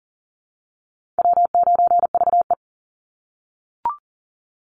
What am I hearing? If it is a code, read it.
Morse code, heard as a keyed tone, about 40 wpm, 700 Hz. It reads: W9VE
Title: Call Signs - 40wpm